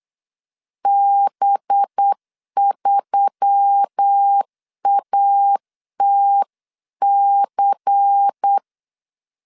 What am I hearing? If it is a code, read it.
Morse code, heard as a keyed tone, about 8 wpm, 800 Hz. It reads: B3ATC
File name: CW / DET / 1 6 1500C S B 8wpm 5142